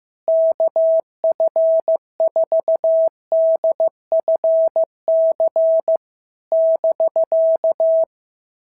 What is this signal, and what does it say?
Morse code, heard as a keyed tone, about 15 wpm, 650 Hz. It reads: KF4DFC <BK>
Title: SOTA QSO Practice 15wpm